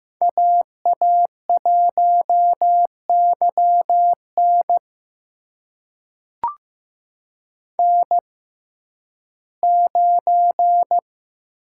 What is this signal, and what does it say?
Morse code, heard as a keyed tone, about 15 wpm, spaced as for 11 wpm, 700 Hz. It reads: AA1YN N9
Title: USA Call Signs - 5 Characters - 6x Character Spacing - 15wpm